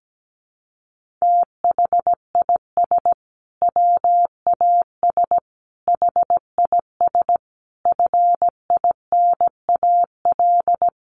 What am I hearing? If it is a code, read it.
Morse code, heard as a keyed tone, about 17 wpm, 700 Hz. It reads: THIS WAS HIS FINAL
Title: Sentences from Top 1300 Words 17wpm